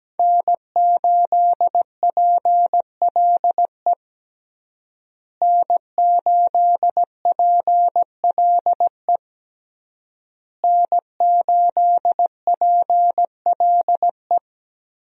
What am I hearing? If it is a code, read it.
Morse code, heard as a keyed tone, about 17 wpm, 700 Hz. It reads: N8PLE N8PLE N8PLE
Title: USA Call Signs - 5 Characters - Repeated 3x - 17wpm